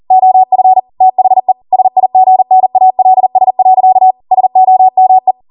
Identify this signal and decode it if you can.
Morse code, heard as a keyed tone, about 40 wpm, 750 Hz. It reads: OF THE SIGNALS. SOME